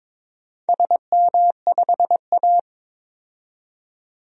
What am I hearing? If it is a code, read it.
Morse code, heard as a keyed tone, about 22 wpm, 700 Hz. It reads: SM5A
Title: DXCC-100 Call Signs - 4 Characters - 22wpm